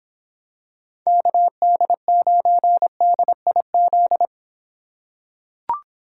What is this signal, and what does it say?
Morse code, heard as a keyed tone, about 24 wpm, 700 Hz. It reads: KD9DIZ